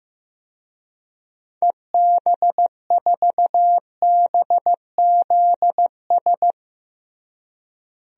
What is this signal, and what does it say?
Morse code, heard as a keyed tone, about 15 wpm, 700 Hz. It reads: EB4BZS
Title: DXCC-100 Call Signs - 6 Characters - Repeated 3x - 15wpm